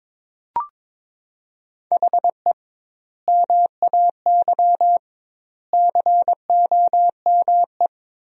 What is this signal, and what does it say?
Morse code, heard as a keyed tone, about 22 wpm, 700 Hz. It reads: HE MAY COME
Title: 2x Word-Spacing - Sentences from Top 300 Words - 22wpm